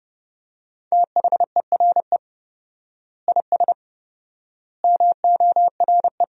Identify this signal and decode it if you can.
Morse code, heard as a keyed tone, about 30 wpm, 700 Hz. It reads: THERE IS MORE
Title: 4x Word-Spacing - Sentences from Top 200 Words - 30wpm